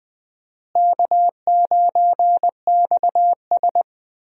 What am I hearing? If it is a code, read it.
Morse code, heard as a keyed tone, about 20 wpm, 700 Hz. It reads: K9XS
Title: USA Call Signs - 4 Characters - Repeated 3x - 20wpm